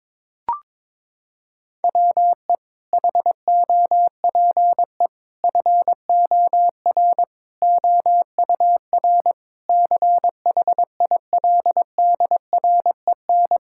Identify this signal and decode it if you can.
Morse code, heard as a keyed tone, about 22 wpm, 700 Hz. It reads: WE HOPE FOR OUR CHILDREN
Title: Sentences from Top 1000 Words - Review Part 6 - 22wpm